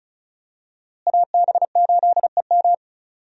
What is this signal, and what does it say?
Morse code, heard as a keyed tone, about 36 wpm, 700 Hz. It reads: AB8EM
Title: USA Call Signs - 5 Characters - Repeated 2x - 35wpm